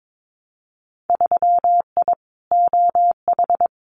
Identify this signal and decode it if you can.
Morse code, heard as a keyed tone, about 22 wpm, 700 Hz. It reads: 3I OH
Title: Winter Field Day Exchanges Encore 22wpm